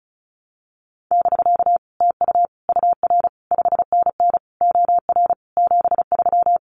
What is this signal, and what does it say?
Morse code, heard as a keyed tone, about 36 wpm, 700 Hz. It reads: <BK> TU UR 5NN OR 73
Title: Common QSO Phrases Encore 35wpm